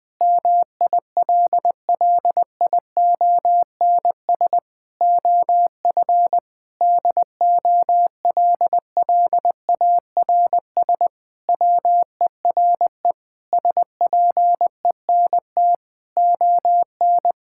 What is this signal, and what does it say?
Morse code, heard as a keyed tone, about 21 wpm, 700 Hz. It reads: MILLIONS OF DOLLARS WERE SPENT ON
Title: Sentences with 5-Word Phrases - 20wpm